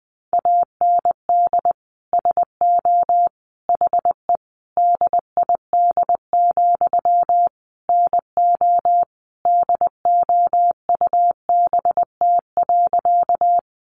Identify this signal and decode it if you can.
Morse code, heard as a keyed tone, about 20 wpm, 700 Hz. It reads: AND SO HE DID, NO DOUBT.
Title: Sentences from Top 1500 Words 20wpm